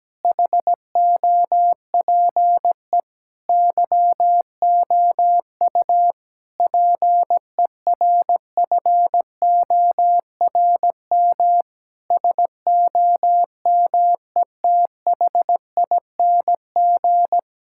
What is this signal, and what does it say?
Morse code, heard as a keyed tone, about 17 wpm, 700 Hz. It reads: HOPE YOU PERFORM SOMETHING
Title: Sentences from Top 1000 Words - Review Part 6 - 17wpm